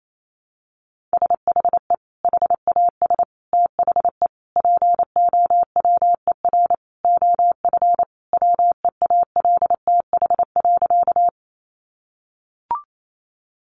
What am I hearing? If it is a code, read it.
Morse code, heard as a keyed tone, about 28 wpm, 700 Hz. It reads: SHE HAS THE POWER OF WEALTH.
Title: Sentences from Top 1200 Words 28wpm